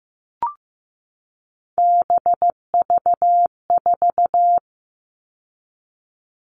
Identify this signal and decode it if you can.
Morse code, heard as a keyed tone, about 15 wpm, 700 Hz. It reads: BV4
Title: All Cut Numbers - 3-Digit 15wpm